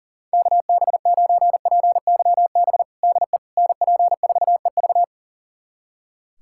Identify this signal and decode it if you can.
Morse code, heard as a keyed tone, about 40 wpm, 700 Hz. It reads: KB9PYB DE NP4EV
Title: USA Callsign DE Callsign 40wpm